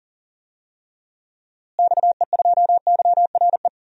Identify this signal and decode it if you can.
Morse code, heard as a keyed tone, about 40 wpm, 700 Hz. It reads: XE2YRE
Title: DXCC-100 Call Signs - 6 Characters - Repeated 2x - 40wpm